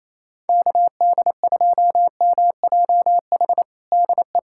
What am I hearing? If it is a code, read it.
Morse code, heard as a keyed tone, about 28 wpm, 700 Hz. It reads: KD2MJH DE
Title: USA Callsign DE Callsign 28wpm